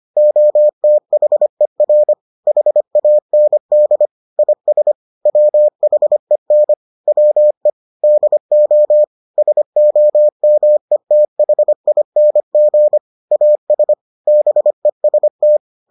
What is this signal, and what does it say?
Morse code, heard as a keyed tone, about 25 wpm, 600 Hz. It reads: OTHER HAND IS WHEN WE DO SOMETHING AS BEST